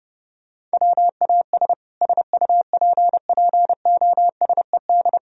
Morse code, heard as a keyed tone, about 31 wpm, 700 Hz. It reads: WAS SUPPOSED